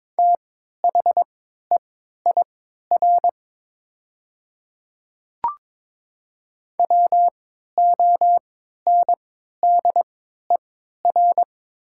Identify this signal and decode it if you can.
Morse code, heard as a keyed tone, about 22 wpm, spaced as for 12 wpm, 700 Hz. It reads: THEIR WONDER
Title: 3x Character Spacing - Top 500 Words - 22wpm